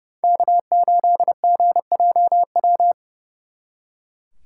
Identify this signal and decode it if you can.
Morse code, heard as a keyed tone, about 39 wpm, 700 Hz. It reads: K8GJW